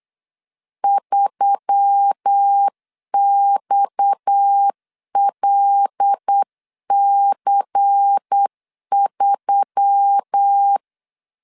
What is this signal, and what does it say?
Morse code, heard as a keyed tone, about 8 wpm, 800 Hz. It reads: 3XLC3
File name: CW / DET / 1 6 1500C S C 8wpm 5219